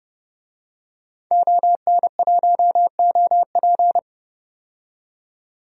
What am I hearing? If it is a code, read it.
Morse code, heard as a keyed tone, about 30 wpm, 700 Hz. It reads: ON1OP
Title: DXCC-100 Call Signs - 5 Characters - 30wpm